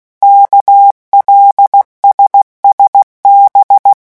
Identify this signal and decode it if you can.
Morse code, heard as a keyed tone, about 16 wpm, 800 Hz. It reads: KLSSB
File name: morse mp3 / revision5-vitesse 15 mots